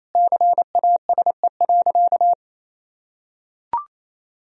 Morse code, heard as a keyed tone, about 28 wpm, 700 Hz. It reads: CASE.
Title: Sentences from Top 1000 Words - Review Part 3 - 28wpm